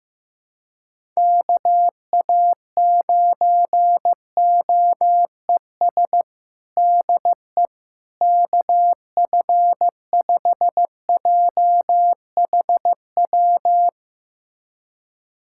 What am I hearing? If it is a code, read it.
Morse code, heard as a keyed tone, about 15 wpm, 700 Hz. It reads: KA9OES DE KF5JHW
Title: USA Callsign DE Callsign 15wpm